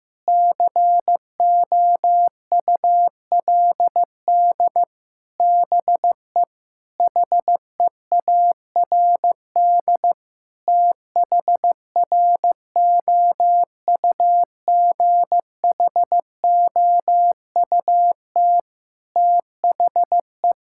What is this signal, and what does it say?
Morse code, heard as a keyed tone, about 15 wpm, 700 Hz. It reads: COULD BE HEARD THROUGHOUT THE
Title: Sentences from Top 1000 Words - Review Part 5 - 15wpm